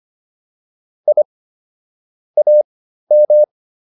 Morse code, heard as a keyed tone, about 25 wpm, spaced as for 13 wpm, 600 Hz. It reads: I AM